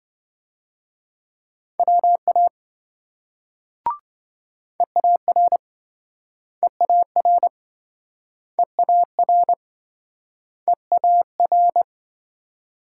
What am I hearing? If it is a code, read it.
Morse code, heard as a keyed tone, about 28 wpm, 700 Hz. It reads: WA EAR EAR EAR EAR